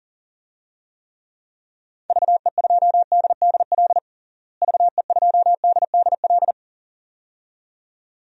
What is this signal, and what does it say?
Morse code, heard as a keyed tone, about 40 wpm, 700 Hz. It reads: VE2DDL VE2DDL
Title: DXCC-100 Call Signs - 6 Characters - Repeated 2x - 40wpm